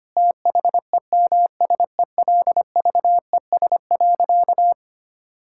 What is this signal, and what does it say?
Morse code, heard as a keyed tone, about 25 wpm, 700 Hz. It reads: THEMSELVES.
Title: Sentences from Top 1000 Words - Review Part 5 - 25wpm